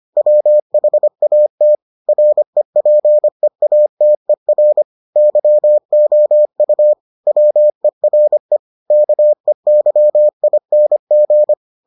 Morse code, heard as a keyed tone, about 25 wpm, 600 Hz. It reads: WHAT REPEATER YOU WERE KEYING